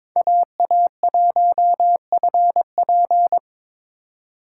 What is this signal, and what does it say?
Morse code, heard as a keyed tone, about 21 wpm, 700 Hz. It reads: AA1FP
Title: USA Call Signs - 2x Character Spacing - 22wpm